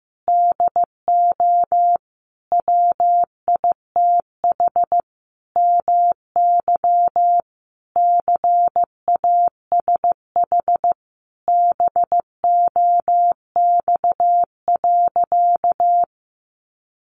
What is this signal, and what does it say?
Morse code, heard as a keyed tone, about 14 wpm, 700 Hz. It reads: DO WITH MY CASH BOX.
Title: Sentences from Top 700 Words 15wpm